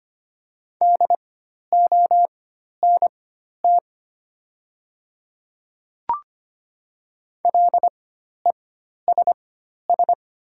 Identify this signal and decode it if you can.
Morse code, heard as a keyed tone, about 25 wpm, spaced as for 12 wpm, 700 Hz. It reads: DONT LESS
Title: 4x Character Spacing - Top 500 Words - 25wpm